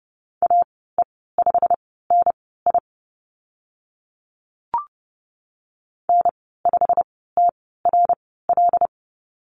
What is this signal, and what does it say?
Morse code, heard as a keyed tone, about 30 wpm, spaced as for 17 wpm, 700 Hz. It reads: AE5NI N5TRL
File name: Lesson 15g - Number 5 - Call Signs 30wpm